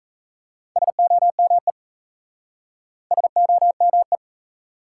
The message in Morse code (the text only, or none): SOME SOME